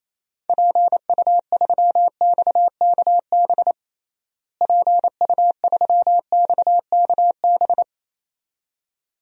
Morse code, heard as a keyed tone, about 27 wpm, 700 Hz. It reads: PU3XKB PU3XKB
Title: DXCC-100 Call Signs - 6 Characters - Repeated 2x - 28wpm